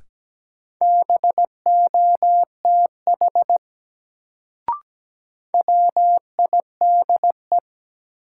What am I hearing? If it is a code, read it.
Morse code, heard as a keyed tone, about 17 wpm, 700 Hz. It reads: BOTH WIDE
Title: Sets of 4 Words - New Word Drill - 17wpm